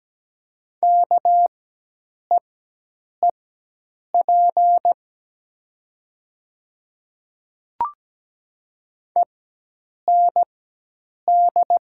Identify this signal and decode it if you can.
Morse code, heard as a keyed tone, about 17 wpm, spaced as for 8 wpm, 700 Hz. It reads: KEEP END
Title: 4x Character Spacing - Top 300 Words - 17wpm